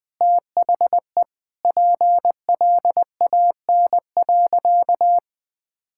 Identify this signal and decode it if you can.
Morse code, heard as a keyed tone, about 20 wpm, 700 Hz. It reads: THE PLAN.
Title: Sentences with 4-Word Phrases - Spoken Beginning - 20wpm